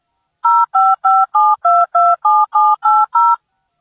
DTMF, heard as telephone touch tones, tones of about 220 ms. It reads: 0557227780